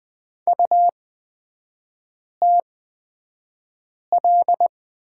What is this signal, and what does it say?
Morse code, heard as a keyed tone, about 20 wpm, spaced as for 5 wpm, 700 Hz. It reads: UTL
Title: 250 Lesson 40e - BK - Three Characters 20wpm